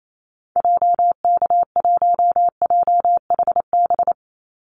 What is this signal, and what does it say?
Morse code, heard as a keyed tone, about 28 wpm, 700 Hz. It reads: JK1JHB
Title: Call Signs Repeated 2x - 28wpm